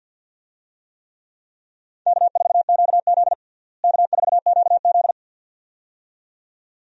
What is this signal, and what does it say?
Morse code, heard as a keyed tone, about 50 wpm, 700 Hz. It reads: K4QZ K4QZ
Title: USA Call Signs - 4 Characters - Repeated 2x - 50wpm